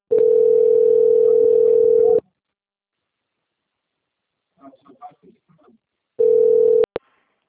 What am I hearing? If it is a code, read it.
Ringback tone (North America), 440 + 480 Hz, with speech.
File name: out- - - - - .2109